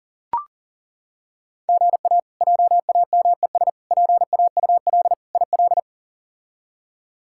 Morse code, heard as a keyed tone, about 40 wpm, 700 Hz. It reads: GA JAMES PAUL IL
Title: K1USN SST QSO Practice - 40wpm